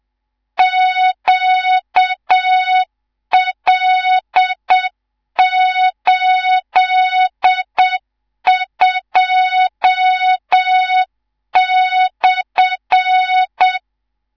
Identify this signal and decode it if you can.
Morse code, heard as a keyed tone, about 7 wpm, 3700 Hz, with noise. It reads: QL82/
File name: Morze-HA2MN / hang / 19 GYAKORLAT